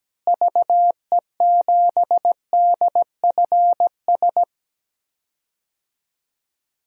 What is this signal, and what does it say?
Morse code, heard as a keyed tone, about 17 wpm, 700 Hz. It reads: VE7DFS
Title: DXCC-100 Call Signs - 6 Characters - Repeated 3x - 17wpm